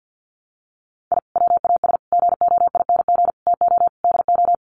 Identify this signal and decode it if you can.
Morse code, heard as a keyed tone, about 50 wpm, 700 Hz. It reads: I WAS GOING TO DO
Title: Sets of 5 Words Encore2 50wpm